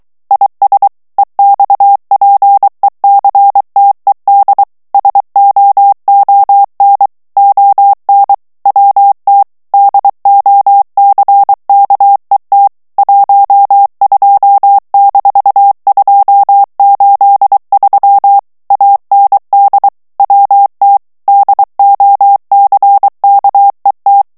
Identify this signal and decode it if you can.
Morse code, heard as a keyed tone, about 23 wpm, 800 Hz. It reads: IS EXPECTED SOON ON WT DOCKET 12-283 AND WT DOCKET
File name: ANGLAIS texte clair 3 vitesse 22 mots minute